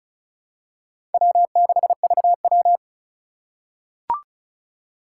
Morse code, heard as a keyed tone, about 34 wpm, 700 Hz. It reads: W6VW
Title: USA Call Signs - 4 Characters - 4x Character Spacing - 35wpm